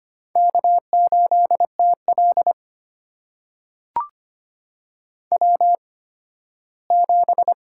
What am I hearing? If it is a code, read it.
Morse code, heard as a keyed tone, about 25 wpm, spaced as for 15 wpm, 700 Hz. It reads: K8TL W7
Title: USA Call Signs - 4 Characters - 8x Character Spacing - 25wpm